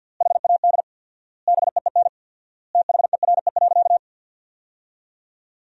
Morse code, heard as a keyed tone, about 50 wpm, 700 Hz. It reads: HAD BEEN THERE.
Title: 4x Word-Spacing - Sentences from Top 300 Words - 50wpm